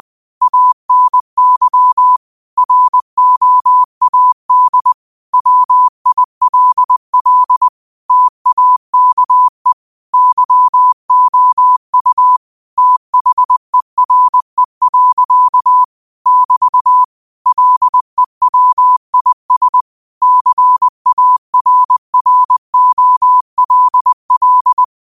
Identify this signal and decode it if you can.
Morse code, heard as a keyed tone, about 20 wpm, 1000 Hz. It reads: ANY ROAD WILL TAKE YOU THERE. <BT> LEWIS CARROLL
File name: 20wpm